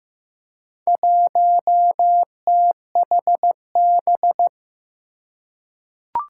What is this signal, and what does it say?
Morse code, heard as a keyed tone, about 15 wpm, 700 Hz. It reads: 1THB
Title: Four Challenging Letter-Number Combinations 15wpm